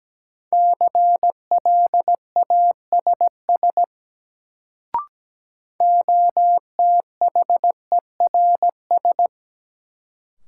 Morse code, heard as a keyed tone, about 17 wpm, 700 Hz. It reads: CLASS OTHERS
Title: Sentences from Top 1000 Words - Review Part 2 - New Word Drill - 17wpm